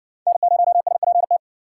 Morse code, heard as a keyed tone, about 60 wpm, 700 Hz. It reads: N0SPT